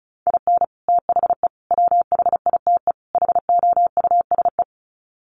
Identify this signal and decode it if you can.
Morse code, heard as a keyed tone, about 34 wpm, 700 Hz. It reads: IN THE WHITE HOUSE
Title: Sets of 4 Words 35wpm